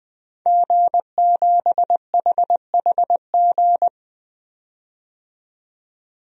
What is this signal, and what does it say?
Morse code, heard as a keyed tone, about 20 wpm, 700 Hz. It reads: G7HHG
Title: DXCC-100 Call Signs - 5 Characters - 20wpm